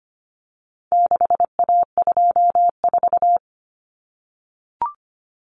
Morse code, heard as a keyed tone, about 25 wpm, 700 Hz. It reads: 6A24
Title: Common Cut Numbers - 4-Digit 25wpm